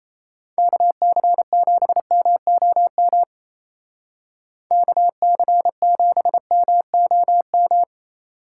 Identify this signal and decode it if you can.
Morse code, heard as a keyed tone, about 35 wpm, 700 Hz. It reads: KC7MOM KC7MOM
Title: Speed Racing - USA 2x3 Call Signs 28wpm